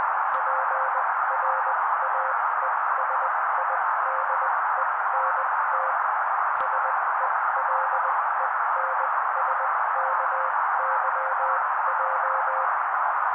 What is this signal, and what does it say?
Morse code, heard as a keyed tone, about 20 wpm, spaced as for 16 wpm, 530 Hz. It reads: PRAESIDENT SELENSKYJ
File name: daily / 16wpm noise